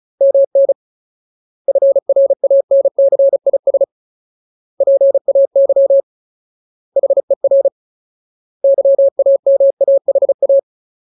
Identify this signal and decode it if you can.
Morse code, heard as a keyed tone, about 34 wpm, 550 Hz. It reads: MN FRANCIS PAY HER YAMAHA
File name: Ultimate Word List / 35WPM Ultimate